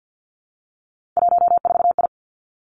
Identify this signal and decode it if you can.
Morse code, heard as a keyed tone, about 50 wpm, 700 Hz. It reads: 14I
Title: Three Challenging Letter-Number Combinations 50wpm